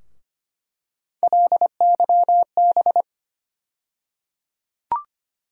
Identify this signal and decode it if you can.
Morse code, heard as a keyed tone, about 25 wpm, 700 Hz. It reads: LYB
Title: USA Call Sign Postfixes - 3 Letters 25wpm